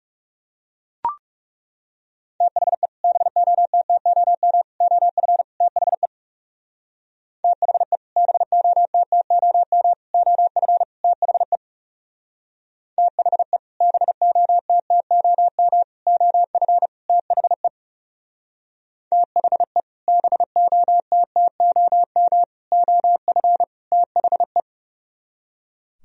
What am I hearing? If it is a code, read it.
Morse code, heard as a keyed tone, about 36 wpm, 700 Hz. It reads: THE BOTTOM OF THE THE BOTTOM OF THE THE BOTTOM OF THE THE BOTTOM OF THE
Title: Speed-Racing - Sets of 4 Words 30wpm